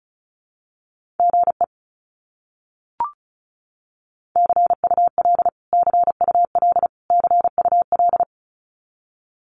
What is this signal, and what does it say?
Morse code, heard as a keyed tone, about 34 wpm, 700 Hz. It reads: GE CUL CUL CUL
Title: Abbreviations Repeated 3x - 35wpm